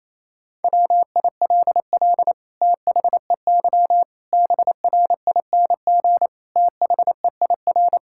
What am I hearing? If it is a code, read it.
Morse code, heard as a keyed tone, about 28 wpm, 700 Hz. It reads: WILL THEY BRING THEIR
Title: Sentences from Top 1000 Words - Review Part 6 - 28wpm